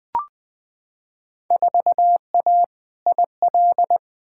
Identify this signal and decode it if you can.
Morse code, heard as a keyed tone, about 20 wpm, 700 Hz. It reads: 4A IL
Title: Field Day Exchanges - 20wpm